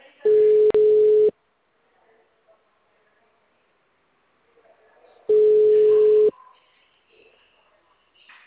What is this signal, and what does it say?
Ringback tone (Europe, 425 Hz).